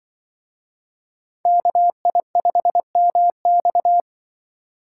Morse code, heard as a keyed tone, about 24 wpm, 700 Hz. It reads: KI5MX
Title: Speed Racing - USA 2x2 Call Signs 20wpm